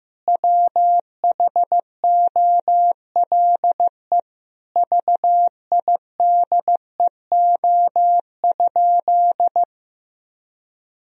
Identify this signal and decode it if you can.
Morse code, heard as a keyed tone, about 15 wpm, 700 Hz. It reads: WHOLE VIDEO?
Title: Sentences from Top 1000 Words - Review Part 4 - 15wpm